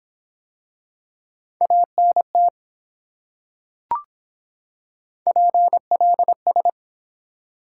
Morse code, heard as a keyed tone, about 24 wpm, 700 Hz. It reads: ANT PLS